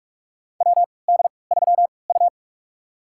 Morse code, heard as a keyed tone, about 44 wpm, spaced as for 25 wpm, 700 Hz. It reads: WD3U
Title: USA Call Signs - 4 Characters - 3x Character Spacing - 45wpm